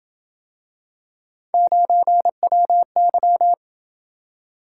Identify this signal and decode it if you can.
Morse code, heard as a keyed tone, about 26 wpm, 700 Hz. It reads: 9WY